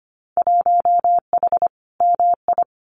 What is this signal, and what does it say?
Morse code, heard as a keyed tone, about 25 wpm, 700 Hz. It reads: 1H MI
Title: Winter Field Day Exchanges 25wpm